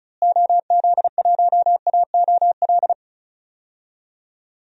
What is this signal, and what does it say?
Morse code, heard as a keyed tone, about 36 wpm, 700 Hz. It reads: OZ1AOL
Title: DXCC-100 Call Signs - 6 Characters - Repeated 2x - 35wpm